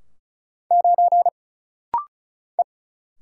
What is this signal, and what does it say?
Morse code, heard as a keyed tone, about 34 wpm, 700 Hz. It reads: 9 E
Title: Single Letter-Number - Rapid-Fire 35wpm